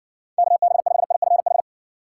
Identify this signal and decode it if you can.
Morse code, heard as a keyed tone, about 53 wpm, 700 Hz. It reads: KB5IFH